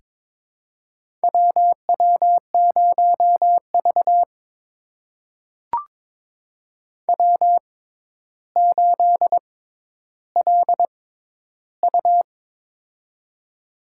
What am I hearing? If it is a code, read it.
Morse code, heard as a keyed tone, about 22 wpm, spaced as for 11 wpm, 700 Hz. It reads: WW0V W8LU
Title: USA Call Signs - 4 Characters - 6x Character Spacing - 22wpm